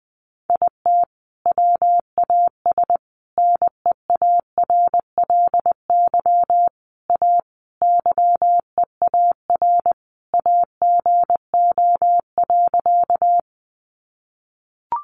Morse code, heard as a keyed tone, about 19 wpm, 700 Hz. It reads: IT WAS NEARLY A YEAR AGO.
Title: Sentences from Top 1300 Words 20wpm